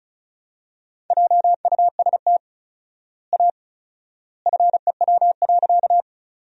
Morse code, heard as a keyed tone, about 36 wpm, 700 Hz. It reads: JUST A FEW.
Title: 4x Word-Spacing - Sentences from Top 300 Words - 35wpm